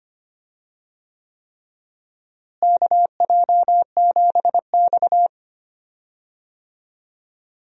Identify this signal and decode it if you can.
Morse code, heard as a keyed tone, about 25 wpm, 700 Hz. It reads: KJ7X
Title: USA Call Signs - 4 Characters - Repeated 3x - 25wpm